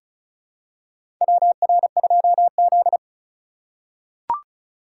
Morse code, heard as a keyed tone, about 36 wpm, 700 Hz. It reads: WR2Z
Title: USA Call Signs - 2x Character Spacing - 35wpm